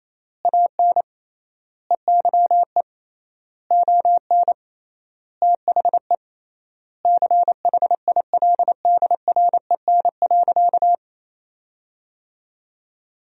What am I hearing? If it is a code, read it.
Morse code, heard as a keyed tone, about 28 wpm, 700 Hz. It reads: AN EYE ON THE CHILDREN.
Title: 3x Word-Spacing - Sentences from Top 300 Words - 28wpm